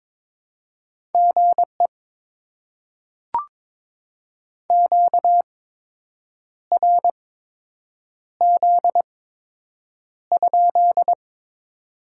Morse code, heard as a keyed tone, about 22 wpm, spaced as for 7 wpm, 700 Hz. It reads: GE QRZ?
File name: Top 90 QSO Elements - 8x Character Spacing - 22wpm